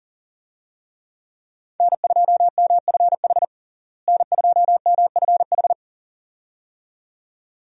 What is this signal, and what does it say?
Morse code, heard as a keyed tone, about 40 wpm, 700 Hz. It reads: N2MFH N2MFH
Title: USA Call Signs - 5 Characters - Repeated 2x - 40wpm